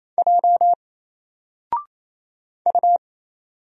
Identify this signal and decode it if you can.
Morse code, heard as a keyed tone, about 27 wpm, 700 Hz. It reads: J U
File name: Single-Letter-Number-28wpm